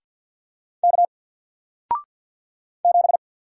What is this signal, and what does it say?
Morse code, heard as a keyed tone, about 50 wpm, 700 Hz. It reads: K 7
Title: Single Letter-Number Alt 50wpm